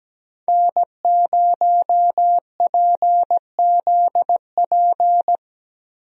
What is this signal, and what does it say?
Morse code, heard as a keyed tone, about 17 wpm, 700 Hz. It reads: N0PZP
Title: USA Call Signs - 5 Characters - Repeated 3x - 17wpm